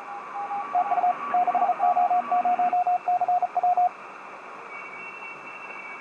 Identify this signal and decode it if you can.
Morse code, heard as a keyed tone, about 38 wpm, 720 Hz, with radio noise. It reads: TU 6O0CW
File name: 6o0cw-on-10m-cw